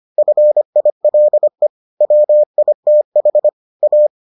Morse code, heard as a keyed tone, about 25 wpm, 600 Hz. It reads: FILE WITH A